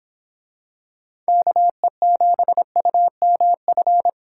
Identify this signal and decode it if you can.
Morse code, heard as a keyed tone, about 26 wpm, 700 Hz. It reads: KE7UMF